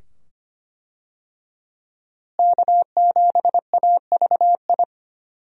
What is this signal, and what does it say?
Morse code, heard as a keyed tone, about 25 wpm, 700 Hz. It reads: K7AVI
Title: USA Call Signs - 5 Characters - Repeated 2x - 25wpm